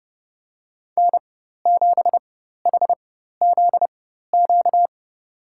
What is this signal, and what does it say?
Morse code, heard as a keyed tone, about 30 wpm, spaced as for 14 wpm, 700 Hz. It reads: N7HZQ
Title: USA Call Signs - 5 Characters - 4x Character Spacing - 30wpm